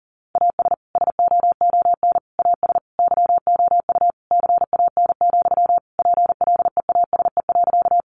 Morse code, heard as a keyed tone, about 40 wpm, 700 Hz. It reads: AS SOON AS YOU CAN, PLEASE.
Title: Sentences from Top 1500 Words 40wpm